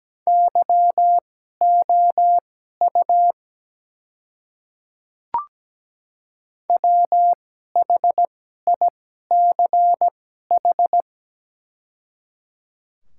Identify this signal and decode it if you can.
Morse code, heard as a keyed tone, about 17 wpm, spaced as for 12 wpm, 690 Hz. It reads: YOU WHICH
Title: 2x Character Spacing - Top 100 Words - 17wpm